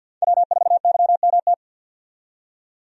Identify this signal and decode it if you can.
Morse code, heard as a keyed tone, about 50 wpm, 700 Hz. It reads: W4YMT
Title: USA Call Signs - 5 Characters - 50wpm